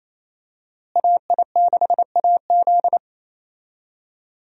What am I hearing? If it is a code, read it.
Morse code, heard as a keyed tone, about 28 wpm, 700 Hz. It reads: AI6AZ
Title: USA Call Signs - 5 Characters - Repeated 2x - 28wpm